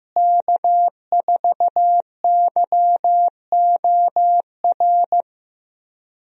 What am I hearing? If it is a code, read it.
Morse code, heard as a keyed tone, about 15 wpm, 700 Hz. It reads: K4YOR
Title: USA Call Signs - 5 Characters - Repeated 2x - 15wpm